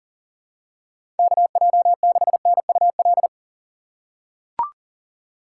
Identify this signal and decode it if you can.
Morse code, heard as a keyed tone, about 40 wpm, 700 Hz. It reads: KJ6NUL
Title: USA Call Signs - 6 Characters - 6x Character Spacing - 40wpm